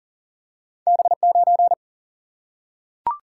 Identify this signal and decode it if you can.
Morse code, heard as a keyed tone, about 42 wpm, 700 Hz. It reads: B9
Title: Single-Letter Single-Number 40wpm